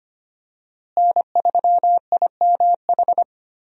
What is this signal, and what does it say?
Morse code, heard as a keyed tone, about 25 wpm, 700 Hz. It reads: N3IMH
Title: USA Call Signs - 5 Characters - Repeated 2x - 25wpm